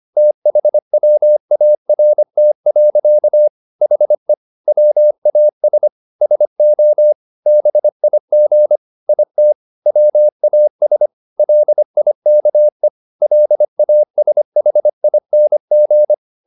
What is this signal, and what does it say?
Morse code, heard as a keyed tone, about 25 wpm, 600 Hz. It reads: THWART. HE WAS SO BIG IT WAS LIKE LASHING